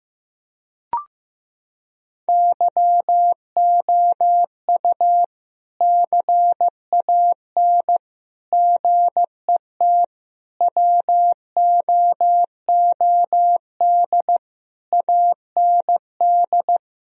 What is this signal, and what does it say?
Morse code, heard as a keyed tone, about 15 wpm, 700 Hz. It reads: YOU CAN GET WOOD AND
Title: Sentences from Top 1000 Words - Review Part 2 - 15wpm